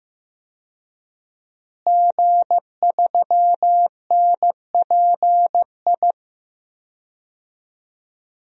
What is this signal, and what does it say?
Morse code, heard as a keyed tone, about 15 wpm, 700 Hz. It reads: G3NPI
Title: DXCC-100 Call Signs - 5 Characters - Repeated 2x - 15wpm